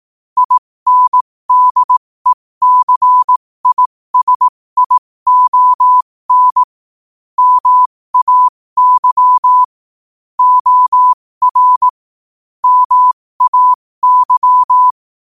Morse code, heard as a keyed tone, about 18 wpm, spaced as for 15 wpm, 1000 Hz. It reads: INDECISION MAY OR MAY
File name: qotdfiles / 15wpm